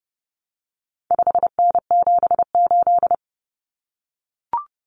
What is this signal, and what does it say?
Morse code, heard as a keyed tone, about 30 wpm, 700 Hz. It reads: 5N78
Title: Common Cut Numbers - 4-Digit 30wpm